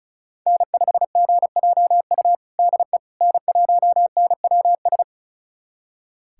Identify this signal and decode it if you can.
Morse code, heard as a keyed tone, about 36 wpm, 700 Hz. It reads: N5GJU DE N1NWS
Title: USA Callsign DE Callsign - 5 Characters - 35wpm